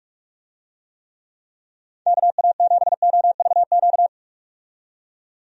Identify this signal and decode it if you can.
Morse code, heard as a keyed tone, about 41 wpm, 700 Hz. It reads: KA7OVQ
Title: USA Call Signs - 6 Characters - 45wpm